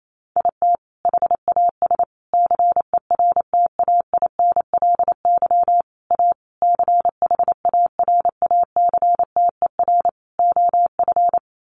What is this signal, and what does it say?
Morse code, heard as a keyed tone, about 28 wpm, 700 Hz. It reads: IT HAS CERTAINLY A CHARACTER OF
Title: Sentences from Top 1200 Words 28wpm